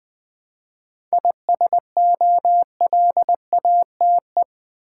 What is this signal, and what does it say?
Morse code, heard as a keyed tone, about 20 wpm, 690 Hz. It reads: ISOLATE
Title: 7 Letter Words - First 3 Letters Given - 20wpm